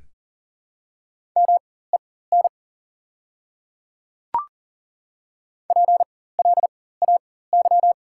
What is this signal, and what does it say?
Morse code, heard as a keyed tone, about 37 wpm, spaced as for 18 wpm, 700 Hz. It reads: MEN PLAY
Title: 4x Character Spacing - Top 200 Words - 40wpm